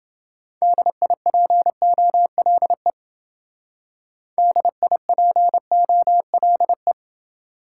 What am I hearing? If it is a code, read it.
Morse code, heard as a keyed tone, about 28 wpm, 700 Hz. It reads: DIPOLE DIPOLE